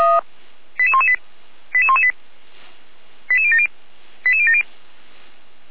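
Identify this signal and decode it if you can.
Five-tone selective call (ZVEI), heard as quick runs of five tones, tones of about 70 ms. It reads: 88118 88118 88070 88070
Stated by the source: ZVEI five-tone call 88192 received by SDR, channel 455, recorded by feuerwehr-orlamuende fire brigade server.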